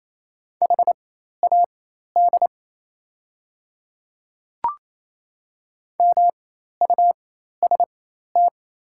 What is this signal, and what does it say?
Morse code, heard as a keyed tone, about 28 wpm, spaced as for 13 wpm, 690 Hz. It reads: HAD MUST
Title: 4x Character Spacing - Top 500 Words - 28wpm